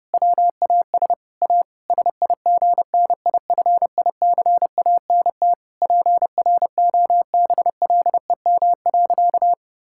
Morse code, heard as a keyed tone, about 30 wpm, 700 Hz. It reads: WAS A SIGNIFICANT PROBLEM.
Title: Sentences from Top 1000 Words - Review Part 6 - 30wpm